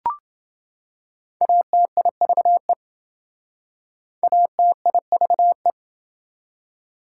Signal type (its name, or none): Morse code